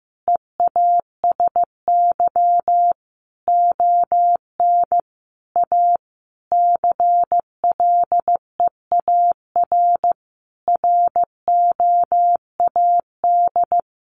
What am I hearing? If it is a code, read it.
Morse code, heard as a keyed tone, about 15 wpm, 700 Hz. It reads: EASY ON A CLEAR ROAD
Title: Sentences from Top 800 Words 15wpm